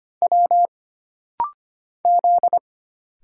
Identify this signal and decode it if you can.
Morse code, heard as a keyed tone, about 25 wpm, 700 Hz. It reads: W Z
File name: Single Character (Letter:Number:Punctuation:Prosign) - Rapid-Fire - 25wpm